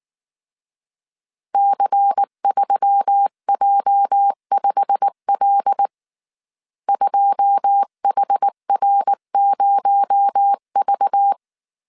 Morse code, heard as a keyed tone, about 19 wpm, 800 Hz. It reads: C3J5L 2HR0V
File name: CW / DET / 1 9 1500C S B 18wpm 2551